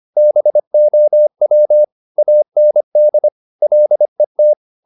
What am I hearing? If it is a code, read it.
Morse code, heard as a keyed tone, about 25 wpm, 600 Hz. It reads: BOW AND LET